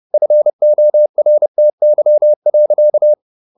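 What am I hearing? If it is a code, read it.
Morse code, heard as a keyed tone, about 30 wpm, 600 Hz. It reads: FORTY.
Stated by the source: Morse code at 30 wpm, the novel Never by Ken Follett in English, from CW med Gnister.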